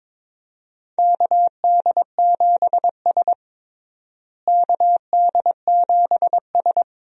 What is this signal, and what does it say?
Morse code, heard as a keyed tone, about 22 wpm, 700 Hz. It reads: KD7S KD7S
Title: USA Call Signs - 4 Characters - Repeated 3x - 22wpm